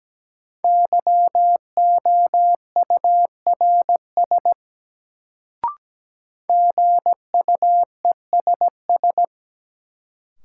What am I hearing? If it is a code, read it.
Morse code, heard as a keyed tone, about 17 wpm, 700 Hz. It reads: YOURS GUESS
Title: Sentences from Top 1500 Words - New Word Drill Alt - 17wpm